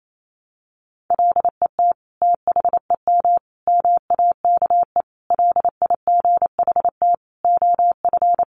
Morse code, heard as a keyed tone, about 28 wpm, 700 Hz. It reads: LET THEM MAKE LIGHT OF
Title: Sentences from Top 1300 Words 28wpm